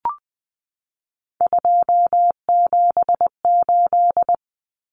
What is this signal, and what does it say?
Morse code, heard as a keyed tone, about 20 wpm, 700 Hz. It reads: 278
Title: Common Cut Numbers - 3-Digit 20wpm